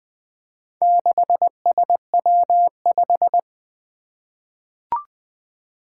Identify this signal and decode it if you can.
Morse code, heard as a keyed tone, about 20 wpm, 700 Hz. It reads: 6SW5
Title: Four Challenging Letter-Number Combinations 20wpm